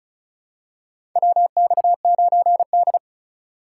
Morse code, heard as a keyed tone, about 36 wpm, 700 Hz. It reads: WX9D
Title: USA Call Signs - 4 Characters - 4x Character Spacing - 35wpm